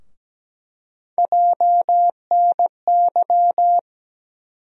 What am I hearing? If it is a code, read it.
Morse code, heard as a keyed tone, about 17 wpm, 700 Hz. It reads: JNY
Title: USA Call Sign Postfixes - 3 Letters 17wpm